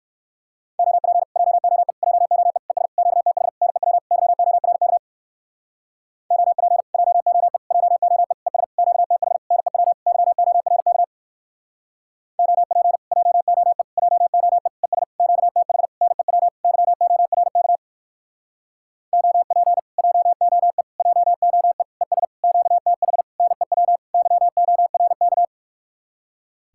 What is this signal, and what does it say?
Morse code, heard as a keyed tone, about 59 wpm, 700 Hz. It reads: OP JOE JOE ES QTH NEW YORK OP JOE JOE ES QTH NEW YORK OP JOE JOE ES QTH NEW YORK OP JOE JOE ES QTH NEW YORK